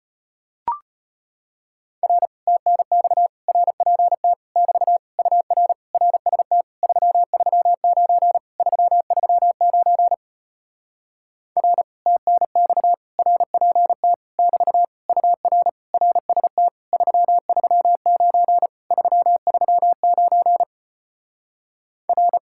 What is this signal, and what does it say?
Morse code, heard as a keyed tone, about 33 wpm, 700 Hz. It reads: R TNX RPT <BT> UR RST 339 339 R TNX RPT <BT> UR RST 339 339 R